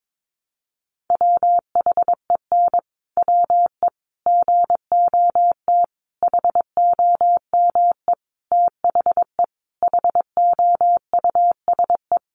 Morse code, heard as a keyed tone, about 22 wpm, 700 Hz. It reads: WHEN WE GOT HOME THE HOUSE
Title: Sentences from Top 500 Words 22wpm